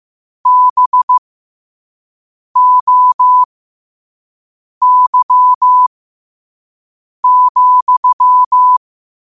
Morse code, heard as a keyed tone, about 15 wpm, spaced as for 5 wpm, 1000 Hz. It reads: BOY,
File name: qotdfiles / 5wpm